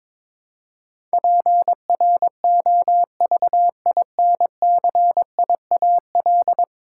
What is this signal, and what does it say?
Morse code, heard as a keyed tone, about 23 wpm, 700 Hz. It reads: PROVINCIAL
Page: morse-code-ninja-practice-22wpm